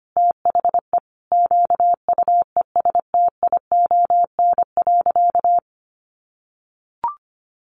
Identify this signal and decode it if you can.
Morse code, heard as a keyed tone, about 25 wpm, 700 Hz. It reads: THE QUESTION.
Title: Sentences from Top 600 Words 25wpm